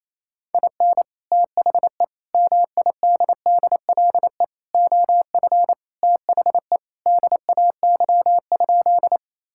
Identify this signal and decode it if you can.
Morse code, heard as a keyed tone, about 28 wpm, 700 Hz. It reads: IN THE MIDDLE OF THE DAY?
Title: Sentences from Top 1000 Words - Review Part 6 - 28wpm